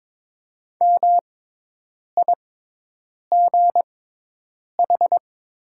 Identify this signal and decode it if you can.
Morse code, heard as a keyed tone, about 22 wpm, spaced as for 8 wpm, 700 Hz. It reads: MIGH
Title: 6x Character Spacing - Top 500 Words - 22wpm